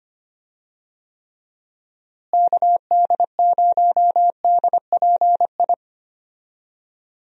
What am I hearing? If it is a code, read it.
Morse code, heard as a keyed tone, about 25 wpm, 700 Hz. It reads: KD0DPI
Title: USA Call Signs - 6 Characters - Repeated 2x - 25wpm